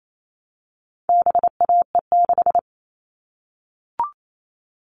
Morse code, heard as a keyed tone, about 28 wpm, 700 Hz. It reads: BAE6
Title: All Cut Numbers - 4-Digit 28wpm